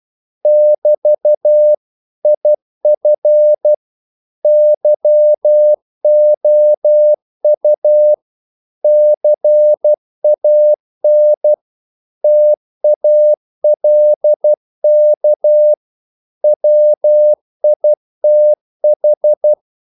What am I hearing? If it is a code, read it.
Morse code, heard as a keyed tone, about 12 wpm, 600 Hz. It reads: <BT> IF YOU CAN TALK WITH
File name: CW / Practice / If12